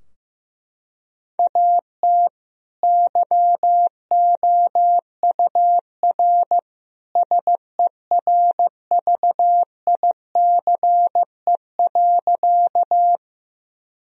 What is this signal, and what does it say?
Morse code, heard as a keyed tone, about 15 wpm, 700 Hz. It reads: AT YOUR SERVICE.
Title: Sentences from Top 1000 Words - Review Part 4 - 15wpm